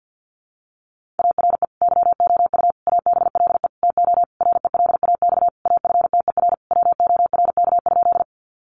Morse code, heard as a keyed tone, about 50 wpm, 700 Hz. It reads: ARE YOU ABLE TO RELAX AFTER WORK?
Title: Sentences from Top 1400 Words 50wpm